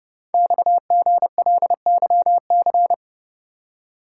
Morse code, heard as a keyed tone, about 28 wpm, 690 Hz. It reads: XGLYC
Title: Code Groups - 5 Letters - 30wpm